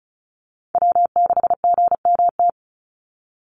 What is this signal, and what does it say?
Morse code, heard as a keyed tone, about 34 wpm, 700 Hz. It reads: W6GMT
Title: Call Signs - 35wpm